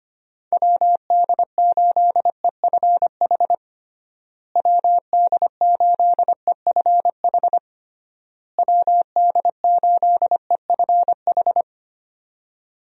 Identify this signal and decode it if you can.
Morse code, heard as a keyed tone, about 25 wpm, 700 Hz. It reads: WD8EFH WD8EFH WD8EFH
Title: USA Call Signs - 6 Characters - Repeated 3x - 25wpm